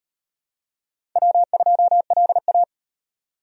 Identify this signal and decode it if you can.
Morse code, heard as a keyed tone, about 36 wpm, 700 Hz. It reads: W2LA